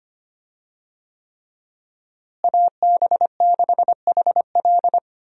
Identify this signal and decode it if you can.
Morse code, heard as a keyed tone, about 25 wpm, 700 Hz. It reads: AB6HL
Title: USA Call Signs - 5 Characters - Repeated 3x - 25wpm